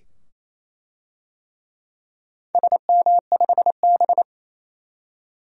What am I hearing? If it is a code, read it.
Morse code, heard as a keyed tone, about 28 wpm, 700 Hz. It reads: SM5B
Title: DXCC-100 Call Signs - 4 Characters - 28wpm